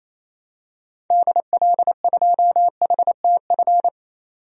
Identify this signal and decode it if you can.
Morse code, heard as a keyed tone, about 28 wpm, 700 Hz. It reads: DL2HTF
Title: DXCC-100 Call Signs - 6 Characters - 28wpm